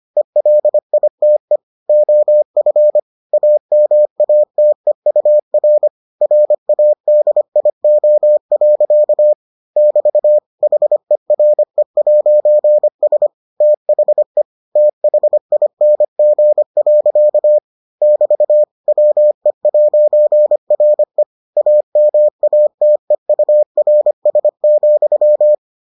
Morse code, heard as a keyed tone, about 25 wpm, 600 Hz. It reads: ELITE OF AMATEUR RADIO. <BT> HERE'S THE THING. <BT> WE'RE AMATEURS,